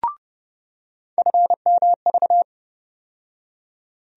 Morse code, heard as a keyed tone, about 30 wpm, 700 Hz. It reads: FMV
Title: USA Call Sign Postfixes - 3 Letters 30wpm